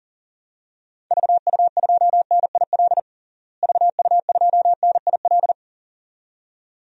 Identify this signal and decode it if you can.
Morse code, heard as a keyed tone, about 40 wpm, 700 Hz. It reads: VU2NIL VU2NIL
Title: DXCC-100 Call Signs - 6 Characters - Repeated 2x - 40wpm